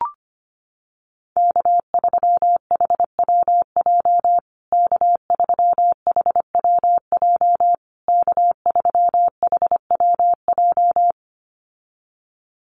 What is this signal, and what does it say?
Morse code, heard as a keyed tone, about 25 wpm, 700 Hz. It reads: K3HWJ K3HWJ K3HWJ
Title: USA Call Signs Repeated 3x 25wpm